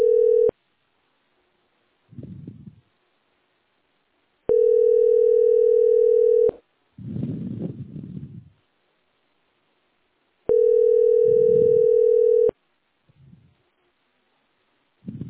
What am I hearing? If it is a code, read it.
Ringback tone (North America), 440 + 480 Hz.
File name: 9c896d41-5495-4931-bf80